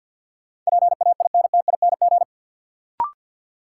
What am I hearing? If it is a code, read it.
Morse code, heard as a keyed tone, about 52 wpm, 700 Hz. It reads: PAINTING
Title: Sentences from Top 1200 Words - New Word Drill Alt - 50wpm